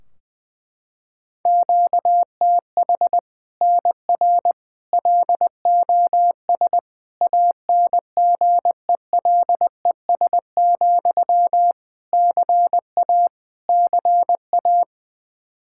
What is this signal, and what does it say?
Morse code, heard as a keyed tone, about 20 wpm, 700 Hz. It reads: QTH NR LOS ANGELES, CA CA
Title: Practice QSO - Exchange 1 - Beginning - 20wpm